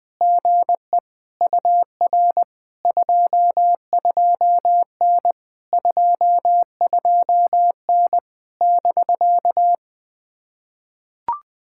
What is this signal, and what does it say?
Morse code, heard as a keyed tone, about 20 wpm, 700 Hz. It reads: GE UR 22N 22N <BK>
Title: POTA QSO Practice - 20wpm